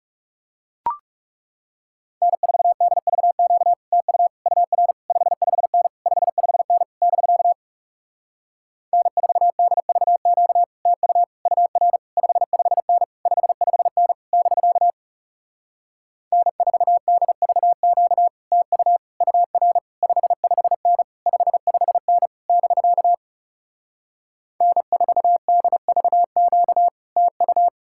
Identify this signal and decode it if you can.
Morse code, heard as a keyed tone, about 37 wpm, 700 Hz. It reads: N4DVQ TU UR 55N 55N <BK> N4DVQ TU UR 55N 55N <BK> N4DVQ TU UR 55N 55N <BK> N4DVQ TU
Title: Speed-Racing Common QSO Phrases Encore - 30wpm